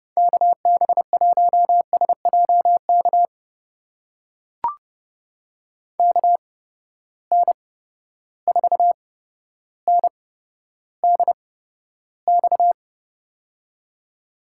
Morse code, heard as a keyed tone, about 30 wpm, spaced as for 13 wpm, 700 Hz. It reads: KB1SJK KN4NDX
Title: USA Call Signs - 6 Characters - 8x Character Spacing - 30wpm